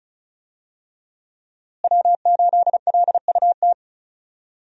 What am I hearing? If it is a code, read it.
Morse code, heard as a keyed tone, about 36 wpm, 700 Hz. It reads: W8LUT
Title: USA Call Signs - 5 Characters - Repeated 3x - 35wpm